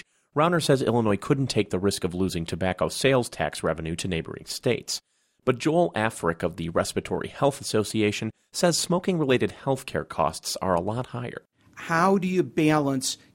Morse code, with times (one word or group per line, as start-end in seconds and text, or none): none